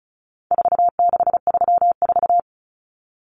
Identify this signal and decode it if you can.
Morse code, heard as a keyed tone, about 34 wpm, 700 Hz. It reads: 4634
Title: Numbers - 4-Digit 35wpm